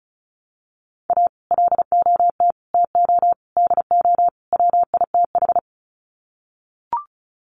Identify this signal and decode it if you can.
Morse code, heard as a keyed tone, about 34 wpm, 700 Hz. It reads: A LOT TO DO WITH
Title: Sets of 5 Words 35wpm